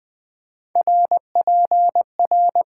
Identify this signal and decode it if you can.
Morse code, heard as a keyed tone, about 20 wpm, 700 Hz. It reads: RPR
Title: Top 90 QSO Elements - 8x Character Spacing - 20wpm